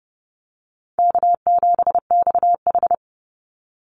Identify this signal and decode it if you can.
Morse code, heard as a keyed tone, about 30 wpm, 700 Hz. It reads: K7XH
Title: Call Signs - 30wpm